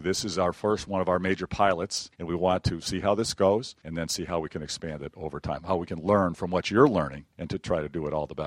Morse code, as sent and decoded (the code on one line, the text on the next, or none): none